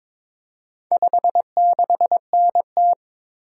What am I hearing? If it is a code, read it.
Morse code, heard as a keyed tone, about 22 wpm, 700 Hz. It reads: 56NT
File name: practice / 22wpm / Four Letter-Number Combinations - 22wpm